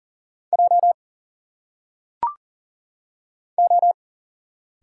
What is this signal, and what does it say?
Morse code, heard as a keyed tone, about 40 wpm, 700 Hz. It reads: J O
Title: Single Letters - Phonetic Pronunciation 40wpm